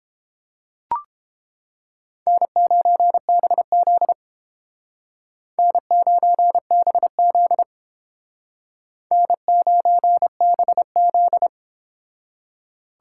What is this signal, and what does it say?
Morse code, heard as a keyed tone, about 30 wpm, 700 Hz. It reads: N9BZ N9BZ N9BZ